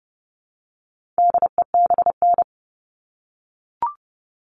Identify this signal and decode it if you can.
Morse code, heard as a keyed tone, about 30 wpm, 700 Hz. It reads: DEBN
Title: All Cut Numbers - 4-Digit 30wpm